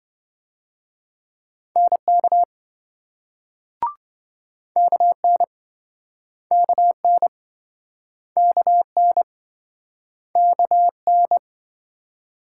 Morse code, heard as a keyed tone, about 27 wpm, 700 Hz. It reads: NK KN KN KN KN